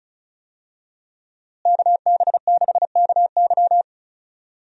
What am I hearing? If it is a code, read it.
Morse code, heard as a keyed tone, about 36 wpm, 700 Hz. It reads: KB6KY
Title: USA Call Signs - 5 Characters - Repeated 3x - 35wpm